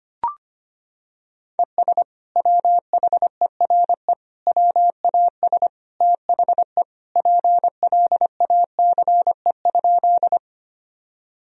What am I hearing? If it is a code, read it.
Morse code, heard as a keyed tone, about 25 wpm, 700 Hz. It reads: ES WHERE WAS THE PLACE?
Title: Sentences from Top 200 Words and Abbreviations 25wpm